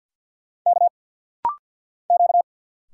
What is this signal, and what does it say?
Morse code, heard as a keyed tone, about 50 wpm, 700 Hz. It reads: K Q
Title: Single Letter-Number Alt - Rapid-Fire 50wpm